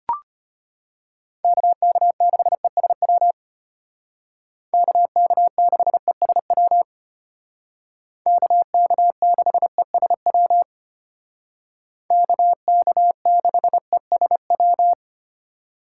Morse code, heard as a keyed tone, about 33 wpm, 700 Hz. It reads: KK6ESW KK6ESW KK6ESW KK6ESW